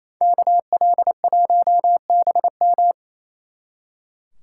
Morse code, heard as a keyed tone, about 28 wpm, 700 Hz. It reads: KL1BM
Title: Speed Racing - USA 2x2 Call Signs 28wpm